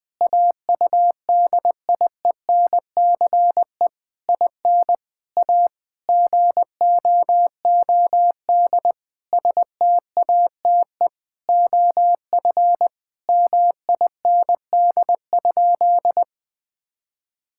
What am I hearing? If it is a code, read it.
Morse code, heard as a keyed tone, about 21 wpm, 700 Hz. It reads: AUDIENCE IN A GOOD STATE OF MIND?
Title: Sentences from Top 1000 Words - Review Part 5 - 20wpm